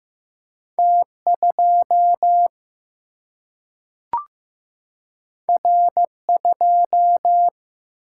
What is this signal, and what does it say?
Morse code, heard as a keyed tone, about 15 wpm, 700 Hz. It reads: T 2 R 2
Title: Single-Letter Single-Number 15wpm